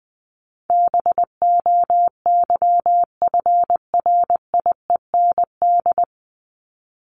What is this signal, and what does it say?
Morse code, heard as a keyed tone, about 20 wpm, 700 Hz. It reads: BOYFRIEND
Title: Top 501-750 Words 20wpm